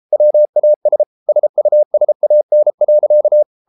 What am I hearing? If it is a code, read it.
Morse code, heard as a keyed tone, about 33 wpm, 600 Hz. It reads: WAS SUSAN.